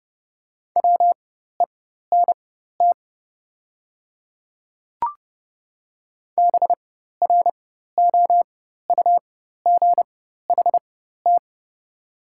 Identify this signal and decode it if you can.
Morse code, heard as a keyed tone, about 30 wpm, spaced as for 14 wpm, 700 Hz. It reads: WENT BROUGHT
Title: 4x Character Spacing - Top 500 Words - 30wpm